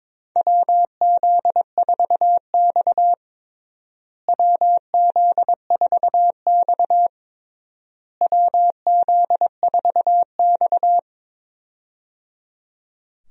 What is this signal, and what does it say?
Morse code, heard as a keyed tone, about 22 wpm, 700 Hz. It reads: WZ4X WZ4X WZ4X
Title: USA Call Signs - 4 Characters - Repeated 3x - 22wpm